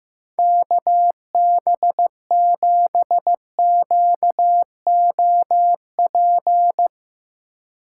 Morse code, heard as a keyed tone, about 16 wpm, 700 Hz. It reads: KB7QOP
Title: K1USN SST QSO Practice - 15wpm